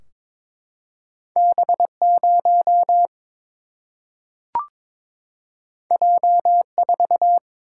Morse code, heard as a keyed tone, about 22 wpm, 700 Hz. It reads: B 0 J 4
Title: Single-Letter Single-Number 22wpm